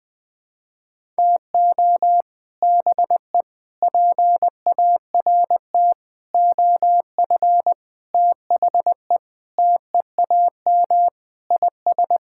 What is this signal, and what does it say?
Morse code, heard as a keyed tone, about 20 wpm, 700 Hz. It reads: TO BE PART OF THE TEAM IS
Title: Sentences with 5-Word Phrases - Spoken Beginning - 20wpm